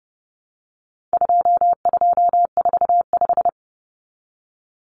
Morse code, heard as a keyed tone, about 30 wpm, 700 Hz. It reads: 2245
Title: Numbers - 4-Digit 30wpm